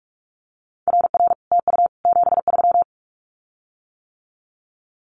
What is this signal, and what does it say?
Morse code, heard as a keyed tone, about 46 wpm, 700 Hz. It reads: RR TU 73
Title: Common QSO Phrases Encore 45wpm